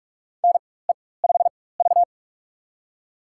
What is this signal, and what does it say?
Morse code, heard as a keyed tone, about 46 wpm, spaced as for 21 wpm, 700 Hz. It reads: NE5V
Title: USA Call Signs - 4 Characters - 4x Character Spacing - 45wpm